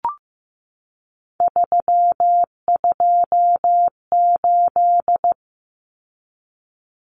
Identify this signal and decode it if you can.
Morse code, heard as a keyed tone, about 15 wpm, 700 Hz. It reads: 328
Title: Common Cut Numbers - 3-Digit 15wpm